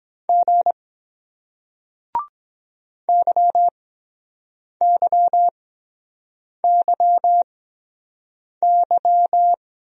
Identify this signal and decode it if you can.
Morse code, heard as a keyed tone, about 23 wpm, spaced as for 7 wpm, 700 Hz. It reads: G YYYY